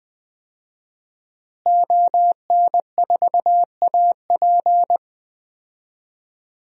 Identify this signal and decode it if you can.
Morse code, heard as a keyed tone, about 20 wpm, 700 Hz. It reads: ON4AP
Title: DXCC-100 Call Signs - 5 Characters - Repeated 2x - 20wpm